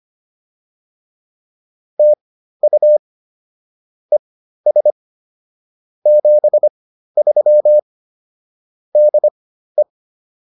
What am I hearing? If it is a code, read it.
Morse code, heard as a keyed tone, about 25 wpm, spaced as for 13 wpm, 600 Hz. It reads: TU ES 73 DE